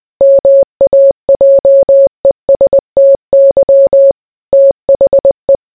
Morse code, heard as a keyed tone, about 20 wpm, 550 Hz. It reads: MAJESTY THE